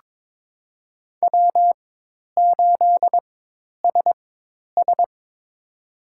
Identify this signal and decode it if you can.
Morse code, heard as a keyed tone, about 21 wpm, spaced as for 10 wpm, 700 Hz. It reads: W8SS
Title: USA Call Signs - 4 Characters - 4x Character Spacing - 22wpm